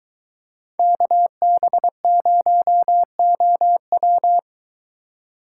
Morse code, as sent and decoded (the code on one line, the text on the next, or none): -.- -... ----- --- .--
KB0OW